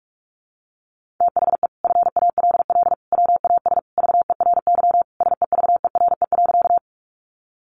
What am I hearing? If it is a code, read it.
Morse code, heard as a keyed tone, about 46 wpm, 700 Hz. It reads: THE FALL WAS VERY SEVERE.
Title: Sentences from Top 1400 Words 45wpm